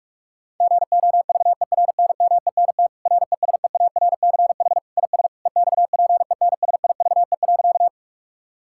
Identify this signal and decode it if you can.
Morse code, heard as a keyed tone, about 46 wpm, 700 Hz. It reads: GOVERNMENT RESEARCH IS EXPENSIVE.
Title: Sentences from Top 1000 Words - Review Part 6 - 45wpm